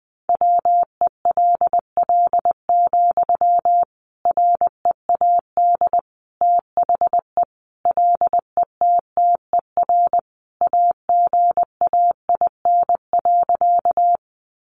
Morse code, heard as a keyed tone, about 19 wpm, 700 Hz. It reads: WELL, READ THE LETTER AGAIN.
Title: Sentences from Top 300 Words 20wpm